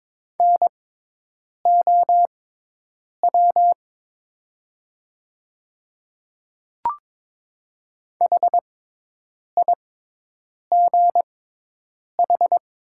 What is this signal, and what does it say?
Morse code, heard as a keyed tone, about 22 wpm, spaced as for 8 wpm, 700 Hz. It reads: NOW HIGH
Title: 6x Character Spacing - Top 200 Words - 22wpm